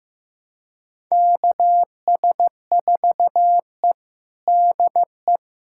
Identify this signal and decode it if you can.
Morse code, heard as a keyed tone, about 15 wpm, 700 Hz. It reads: KS4E DE
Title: USA Callsign DE Callsign - 4 Characters - 15wpm